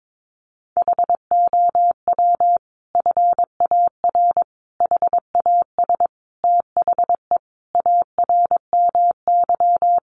Morse code, heard as a keyed tone, about 22 wpm, 700 Hz. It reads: HOW FAR HAS THE ARMY
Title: Sentences from Top 1200 Words 22wpm